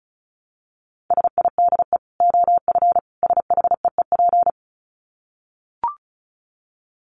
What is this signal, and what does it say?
Morse code, heard as a keyed tone, about 34 wpm, 700 Hz. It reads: SIDE OF SHEEP
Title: Dit Challenge 35wpm